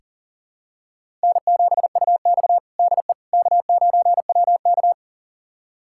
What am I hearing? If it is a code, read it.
Morse code, heard as a keyed tone, about 40 wpm, 700 Hz. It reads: N7UX DE K9WK
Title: USA Callsign DE Callsign - 4 Characters - 40wpm